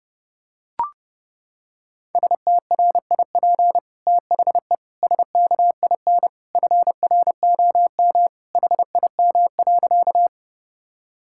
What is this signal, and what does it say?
Morse code, heard as a keyed tone, about 30 wpm, 700 Hz. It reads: STRIP THE SKIN FROM HIM.
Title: Sentences from Top 1600 Words 30wpm